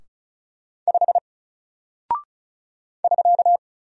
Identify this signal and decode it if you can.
Morse code, heard as a keyed tone, about 36 wpm, 700 Hz. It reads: 5 <SK>
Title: Single Character (Letter:Number:Punctuation:Prosign) 35wpm